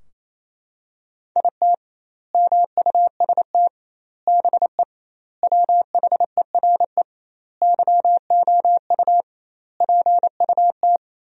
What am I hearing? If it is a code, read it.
Morse code, heard as a keyed tone, about 28 wpm, 700 Hz. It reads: IT MUST BE WHERE YOU PUT
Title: 2x Word-Spacing - Sentences from Top 200 Words - 28wpm